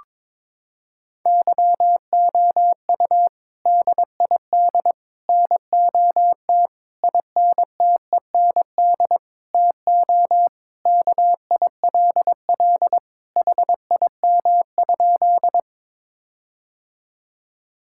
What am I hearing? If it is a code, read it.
Morse code, heard as a keyed tone, about 22 wpm, 700 Hz. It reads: YOU DID NOT INTEND TO KILL HIM?
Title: Sentences from Top 1600 Words 22wpm